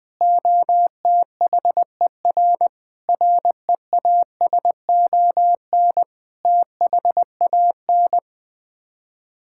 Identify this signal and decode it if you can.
Morse code, heard as a keyed tone, about 21 wpm, 700 Hz. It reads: OTHER REASON THAN
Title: Sets of 6 Words - 20wpm